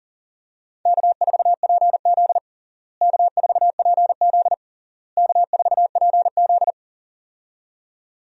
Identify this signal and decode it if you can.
Morse code, heard as a keyed tone, about 37 wpm, 700 Hz. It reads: K4PZ K4PZ K4PZ
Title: USA Call Signs - 4 Characters - Repeated 3x - 40wpm